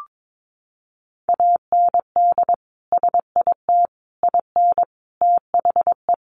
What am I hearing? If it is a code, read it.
Morse code, heard as a keyed tone, about 22 wpm, 700 Hz. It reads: AND SIT IN THE
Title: Sentences from Top 500 Words 22wpm